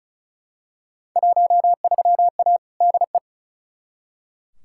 Morse code, heard as a keyed tone, about 36 wpm, 700 Hz. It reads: 13A DE
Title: Field Day Exchanges Encore - 35wpm